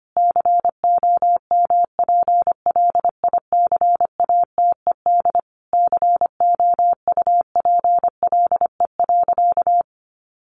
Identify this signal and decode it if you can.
Morse code, heard as a keyed tone, about 25 wpm, 700 Hz. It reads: COMPLICATED COUPLE.
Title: Sentences from Top 1200 Words 25wpm